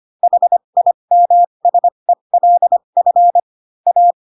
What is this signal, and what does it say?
Morse code, heard as a keyed tone, about 25 wpm, spaced as for 22 wpm, 700 Hz. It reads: HIMSELF A
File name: Pinocchio / Pinocchio ch2 18 25 ik1hge